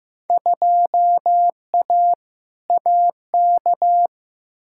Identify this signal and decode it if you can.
Morse code, heard as a keyed tone, about 15 wpm, 700 Hz. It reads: 2A AK
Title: Field Day Exchanges Encore - 15wpm